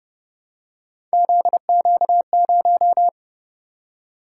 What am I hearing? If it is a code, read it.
Morse code, heard as a keyed tone, about 30 wpm, 700 Hz. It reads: ZQ0